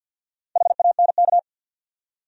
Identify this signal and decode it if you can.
Morse code, heard as a keyed tone, about 49 wpm, 700 Hz. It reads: HANK
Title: USA Common Names - Max 4 Letters - 50wpm